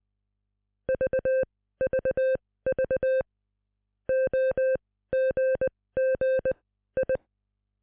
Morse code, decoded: VVV OGGI